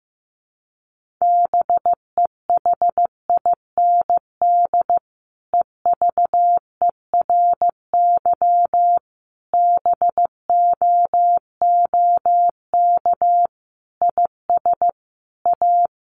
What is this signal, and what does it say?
Morse code, heard as a keyed tone, about 15 wpm, 700 Hz. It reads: BEHIND EVERY BOOK IS A
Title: Sentences from Top 500 Words 15wpm